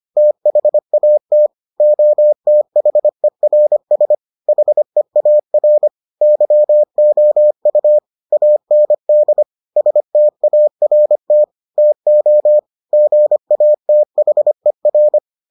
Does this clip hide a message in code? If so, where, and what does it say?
Yes: Morse code from 0.2 to 15.2 s, heard as a keyed tone, about 25 wpm, 600 Hz. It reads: THAT OTHERS HEAR YOU AND START TO GATHER